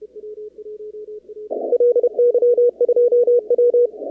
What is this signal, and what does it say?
Morse code, heard as a keyed tone, about 32 wpm, 480 Hz. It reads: LY2W
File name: contests / 2019-cw-wpx-kt5j-k5pi- -2330 0-17